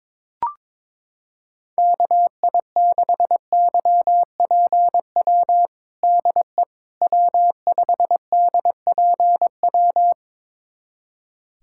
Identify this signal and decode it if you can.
Morse code, heard as a keyed tone, about 22 wpm, 700 Hz. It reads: KI6YPW DE W5DPW
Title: USA Callsign DE Callsign 22wpm